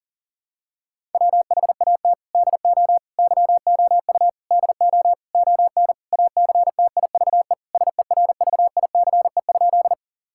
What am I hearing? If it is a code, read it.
Morse code, heard as a keyed tone, about 39 wpm, 700 Hz. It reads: WHAT DO YOU DO ON ACTIVE SERVICE?
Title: Sentences from Top 1000 Words - Review Part 6 - 40wpm